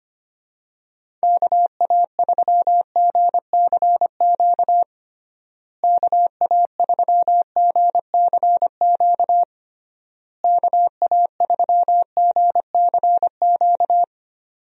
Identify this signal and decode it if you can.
Morse code, heard as a keyed tone, about 25 wpm, 700 Hz. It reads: KA3GCQ KA3GCQ KA3GCQ
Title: USA Call Signs - 6 Characters - Repeated 3x - 25wpm